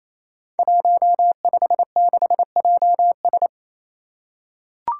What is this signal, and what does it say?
Morse code, heard as a keyed tone, about 29 wpm, 700 Hz. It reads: 156JS
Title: Five Challenging Letter-Number Combinations 28wpm